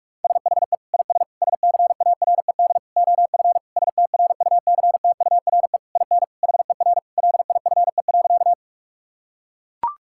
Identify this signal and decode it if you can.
Morse code, heard as a keyed tone, about 43 wpm, 700 Hz. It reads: SHE IS SCARED OF STRUCTURE IN HER LIFE.
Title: Sentences from Top 1000 Words - Review Part 6 - 45wpm